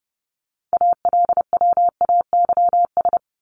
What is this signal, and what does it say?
Morse code, heard as a keyed tone, about 30 wpm, 700 Hz. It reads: ALWAYS
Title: Top 300 Words Repeated 30wpm